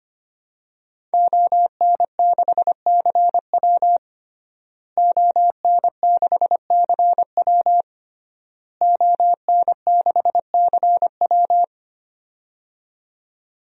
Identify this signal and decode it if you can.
Morse code, heard as a keyed tone, about 25 wpm, 700 Hz. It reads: ON6CW ON6CW ON6CW
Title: DXCC-100 Call Signs - 5 Characters - Repeated 3x - 25wpm